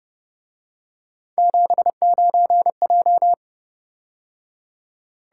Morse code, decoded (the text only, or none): 79J